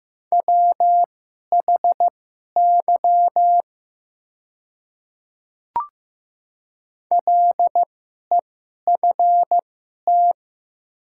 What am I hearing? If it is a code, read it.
Morse code, heard as a keyed tone, about 15 wpm, spaced as for 11 wpm, 700 Hz. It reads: WHY LEFT
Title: 2x Character Spacing - Top 300 Words - 15wpm